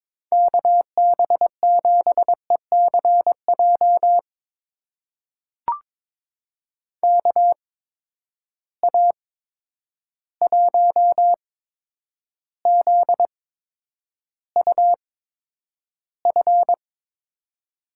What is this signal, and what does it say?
Morse code, heard as a keyed tone, about 23 wpm, spaced as for 10 wpm, 700 Hz. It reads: KB7ECJ KA1ZUF
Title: USA Call Signs - 6 Characters - 8x Character Spacing - 22wpm